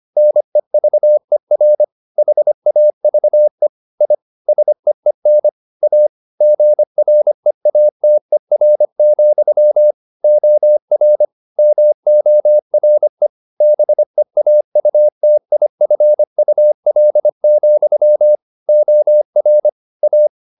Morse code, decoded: NEVER HAVE I SEEN A GREATER, OR MORE BEAUTIFUL, OR A

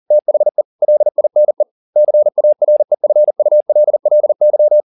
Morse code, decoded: THE LINE CAREFULLY

